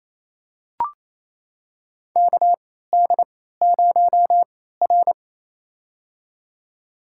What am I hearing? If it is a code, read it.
Morse code, heard as a keyed tone, about 28 wpm, spaced as for 16 wpm, 700 Hz. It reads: KD0R
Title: USA Call Signs - 4 Characters - 3x Character Spacing - 28wpm